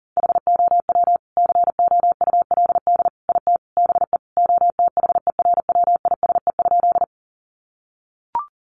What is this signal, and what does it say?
Morse code, heard as a keyed tone, about 40 wpm, 700 Hz. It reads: HOW COULD IT BE OTHERWISE?
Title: Sentences from Top 1200 Words 40wpm